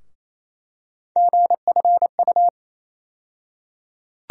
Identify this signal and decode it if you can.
Morse code, heard as a keyed tone, about 28 wpm, 700 Hz. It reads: GFU
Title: USA Call Sign Postfixes - 3 Letters 28wpm